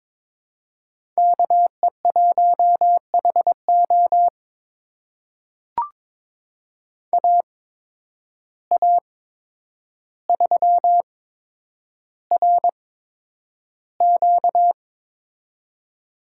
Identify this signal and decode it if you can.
Morse code, heard as a keyed tone, about 22 wpm, spaced as for 9 wpm, 700 Hz. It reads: KE1HO AA3RQ
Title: USA Call Signs - 5 Characters - 8x Character Spacing - 22wpm